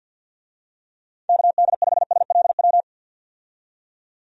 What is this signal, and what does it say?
Morse code, heard as a keyed tone, about 48 wpm, 700 Hz. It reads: KD5SLW
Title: USA Call Signs - 6 Characters - 50wpm